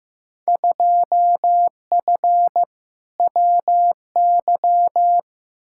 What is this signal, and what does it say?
Morse code, heard as a keyed tone, about 15 wpm, 700 Hz. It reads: 2F WY
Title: Field Day Exchanges Encore - 15wpm